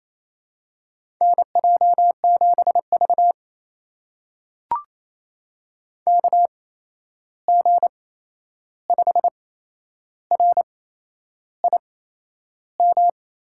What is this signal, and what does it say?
Morse code, heard as a keyed tone, about 28 wpm, spaced as for 11 wpm, 700 Hz. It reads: NJ7V KG5RIM
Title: USA Call Signs - 8x Character Spacing - 28wpm